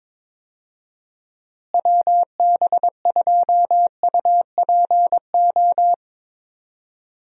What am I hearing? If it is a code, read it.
Morse code, heard as a keyed tone, about 21 wpm, 700 Hz. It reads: WB2UPO
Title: USA Call Signs - 6 Characters - Repeated 3x - 22wpm